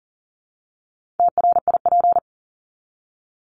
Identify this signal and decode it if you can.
Morse code, heard as a keyed tone, about 40 wpm, 700 Hz. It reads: TRIP
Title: Top 1251-1500 Words 40wpm